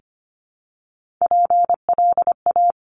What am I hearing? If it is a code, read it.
Morse code, heard as a keyed tone, about 25 wpm, 700 Hz. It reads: PLA
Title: Top Three Letter Combinations in Words 25wpm